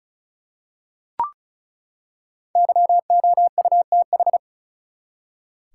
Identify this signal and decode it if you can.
Morse code, heard as a keyed tone, about 34 wpm, 700 Hz. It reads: YOUTH
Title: Five Letter Words - 35wpm